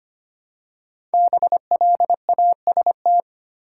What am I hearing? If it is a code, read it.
Morse code, heard as a keyed tone, about 25 wpm, 700 Hz. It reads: BLAST
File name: 5 Letter Words - First 3 Letters Given - 25wpm